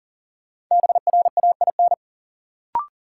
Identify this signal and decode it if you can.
Morse code, heard as a keyed tone, about 40 wpm, 700 Hz. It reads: BRAIN
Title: Sentences from Top 1100 Words - New Word Drill Alt - 40wpm